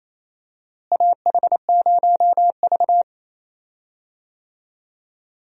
Morse code, heard as a keyed tone, about 28 wpm, 700 Hz. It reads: AH0V
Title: USA Call Signs - 4 Characters - Repeated 3x - 28wpm